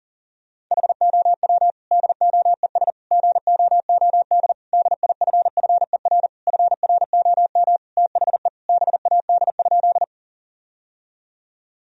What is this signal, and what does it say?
Morse code, heard as a keyed tone, about 40 wpm, 700 Hz. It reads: HOW DOES GOOD DIFFER FROM THE BAD?
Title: Sentences from Top 1000 Words - Review Part 1 - 40wpm